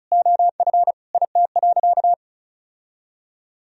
Morse code, heard as a keyed tone, about 34 wpm, 700 Hz. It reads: OF IT.
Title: Sentences from Top 1000 Words - Review Part 4 - 35wpm